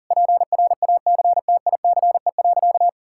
Morse code, heard as a keyed tone, about 40 wpm, 700 Hz. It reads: PRACTICE.
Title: Sentences from Top 1000 Words - Review Part 5 - 40wpm